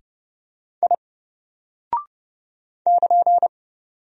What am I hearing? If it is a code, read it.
Morse code, heard as a keyed tone, about 31 wpm, 700 Hz. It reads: I <KN>
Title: Single Character (Letter:Number:Punctuation:Prosign) 30wpm